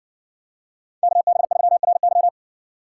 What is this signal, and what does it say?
Morse code, heard as a keyed tone, about 58 wpm, 700 Hz. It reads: KB3RY